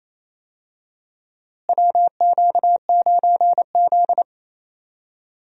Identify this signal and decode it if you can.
Morse code, heard as a keyed tone, about 28 wpm, 700 Hz. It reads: WQ9Z
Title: USA Call Signs - 4 Characters - Repeated 3x - 28wpm